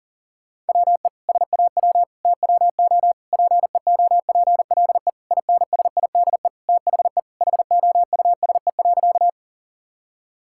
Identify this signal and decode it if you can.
Morse code, heard as a keyed tone, about 40 wpm, 700 Hz. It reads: WE SAW TWO PEOPLE INSIDE THE HOUSE.
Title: Sentences from Top 1000 Words - Review Part 6 - 40wpm